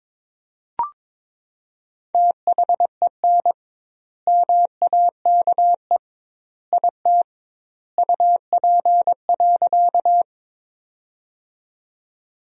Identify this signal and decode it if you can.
Morse code, heard as a keyed tone, about 22 wpm, 700 Hz. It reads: THEN MAKE IT UP.
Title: 2x Word-Spacing - Sentences from Top 100 Words - 22wpm